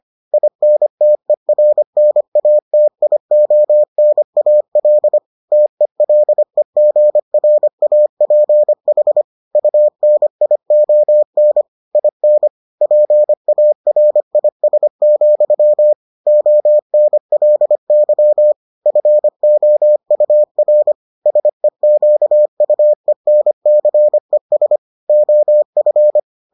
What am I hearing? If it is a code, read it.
Morse code, heard as a keyed tone, about 25 wpm, 600 Hz. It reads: INTERNATIONAL TELEGRAPH UNION IN PARIS, ONLY FOUR SEQUENCES OF